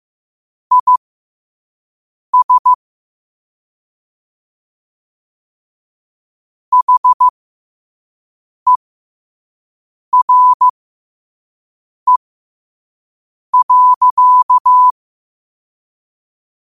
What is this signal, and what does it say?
Morse code, heard as a keyed tone, about 15 wpm, spaced as for 5 wpm, 1000 Hz. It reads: IS HERE.
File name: qotdfiles / 5wpm